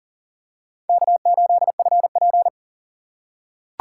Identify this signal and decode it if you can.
Morse code, heard as a keyed tone, about 39 wpm, 690 Hz. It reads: K8FP
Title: USA Call Signs - 4 Characters - 2x Character Spacing - 40wpm